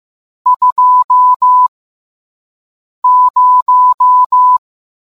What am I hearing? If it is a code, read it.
Morse code, heard as a keyed tone, about 15 wpm, spaced as for 5 wpm, 1000 Hz. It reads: 20